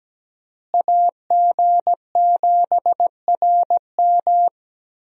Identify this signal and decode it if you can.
Morse code, heard as a keyed tone, about 17 wpm, 700 Hz. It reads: AG7RM
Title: USA Call Signs - 5 Characters - Repeated 3x - 17wpm